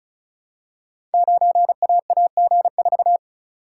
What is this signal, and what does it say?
Morse code, heard as a keyed tone, about 36 wpm, 700 Hz. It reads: 9AAG4
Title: Code Groups - 5 Letter:Numbers - 35wpm